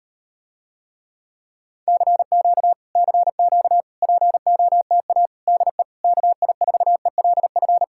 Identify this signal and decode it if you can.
Morse code, heard as a keyed tone, about 38 wpm, 700 Hz. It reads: CQ CQ POTA DE KI4ELF